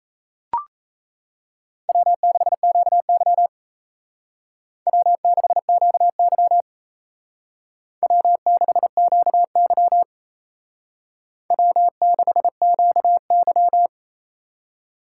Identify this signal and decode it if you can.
Morse code, heard as a keyed tone, about 33 wpm, 700 Hz. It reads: W6QY W6QY W6QY W6QY
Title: Speed Racing - USA 1x2 Call Signs 28wpm